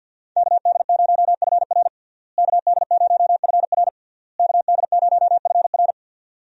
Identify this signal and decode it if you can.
Morse code, heard as a keyed tone, about 46 wpm, 700 Hz. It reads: KD0FR KD0FR KD0FR
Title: USA Call Signs - 5 Characters - Repeated 3x - 50wpm